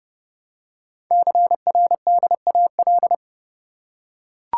Morse code, heard as a keyed tone, about 30 wpm, 700 Hz. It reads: CRDAL
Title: Code Groups - 5 Letters - 30wpm